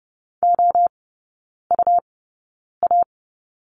Morse code, heard as a keyed tone, about 30 wpm, spaced as for 9 wpm, 700 Hz. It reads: OUA
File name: Lesson 17e - Letter u - Three Characters 30wpm